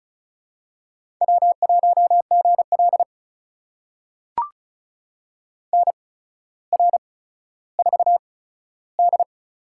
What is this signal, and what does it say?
Morse code, heard as a keyed tone, about 34 wpm, spaced as for 15 wpm, 700 Hz. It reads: W1GL NR4D
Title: USA Call Signs - 4 Characters - 8x Character Spacing - 35wpm